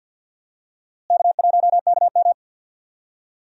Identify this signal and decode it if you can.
Morse code, heard as a keyed tone, about 51 wpm, 700 Hz. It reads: K1KM
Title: USA Call Signs - 4 Characters - Repeated 2x - 50wpm